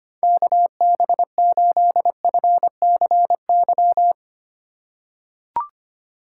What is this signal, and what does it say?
Morse code, heard as a keyed tone, about 25 wpm, 700 Hz. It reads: KB8FCY
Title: USA Call Signs - 6 Characters - 4x Character Spacing - 25wpm